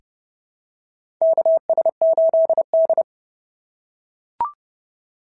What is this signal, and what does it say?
Morse code, heard as a keyed tone, about 32 wpm, 690 Hz. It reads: KS8D
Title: SOTA QSO Practice 30wpm